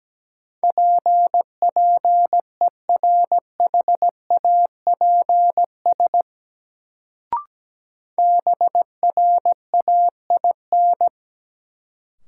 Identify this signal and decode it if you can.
Morse code, heard as a keyed tone, about 17 wpm, 700 Hz. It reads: PPERHAPS BRAIN
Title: Sentences from Top 1100 Words - New Word Drill Alt - 17wpm